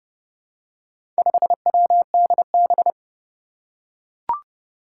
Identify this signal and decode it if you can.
Morse code, heard as a keyed tone, about 30 wpm, 700 Hz. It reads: 5WDB
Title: Four Challenging Letter-Number Combinations 30wpm